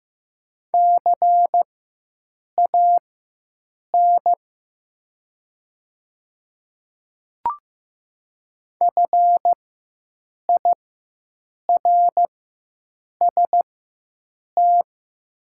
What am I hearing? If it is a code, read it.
Morse code, heard as a keyed tone, about 15 wpm, spaced as for 7 wpm, 700 Hz. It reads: CAN FIRST
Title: 4x Character Spacing - Top 100 Words - 15wpm